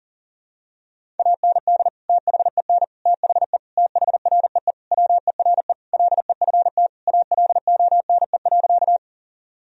Morse code, heard as a keyed tone, about 40 wpm, 700 Hz. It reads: AND THEN THE THREE WERE LEFT ALONE.
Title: Sentences from Top 1000 Words - Review Part 1 - 40wpm